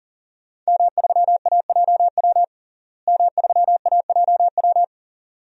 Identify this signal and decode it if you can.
Morse code, heard as a keyed tone, about 40 wpm, 700 Hz. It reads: M3AJW M3AJW
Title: DXCC-100 Call Signs - 5 Characters - Repeated 2x - 40wpm